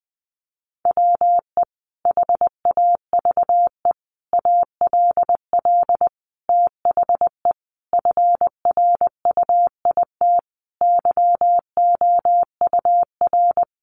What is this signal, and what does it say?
Morse code, heard as a keyed tone, about 20 wpm, 700 Hz. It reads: WE HAVE ALL THE FRUIT YOUR
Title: Sentences from Top 1300 Words 20wpm